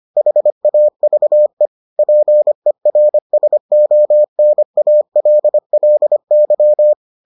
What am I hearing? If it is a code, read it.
Morse code, heard as a keyed tone, about 25 wpm, 600 Hz. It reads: HAVE PERSONALLY